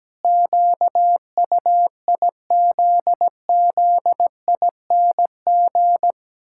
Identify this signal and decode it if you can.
Morse code, heard as a keyed tone, about 17 wpm, 700 Hz. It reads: QUIZZING
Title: Pangram Challenge v2 - 17wpm